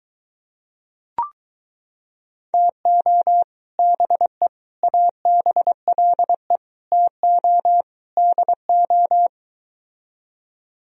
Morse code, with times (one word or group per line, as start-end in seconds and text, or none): 2.5-3.4 TO
3.8-4.5 BE
4.8-6.6 ABLE
6.9-7.8 TO
8.2-9.3 DO